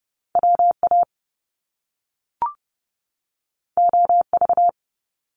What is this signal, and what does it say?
Morse code, heard as a keyed tone, about 30 wpm, 700 Hz. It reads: WA OV
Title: Top Two Letter Combinations in Words 30wpm